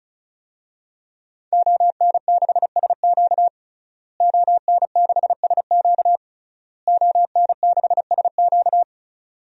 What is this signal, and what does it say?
Morse code, heard as a keyed tone, about 36 wpm, 700 Hz. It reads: ON6SQ ON6SQ ON6SQ
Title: DXCC-100 Call Signs - 5 Characters - Repeated 3x - 35wpm